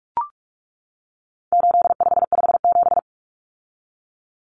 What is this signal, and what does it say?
Morse code, heard as a keyed tone, about 44 wpm, 700 Hz. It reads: 8557
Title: Numbers - 4-Digit 45wpm